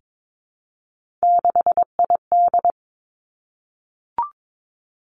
Morse code, heard as a keyed tone, about 22 wpm, 700 Hz. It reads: 6ID
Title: Three Challenging Letter-Number Combinations 22wpm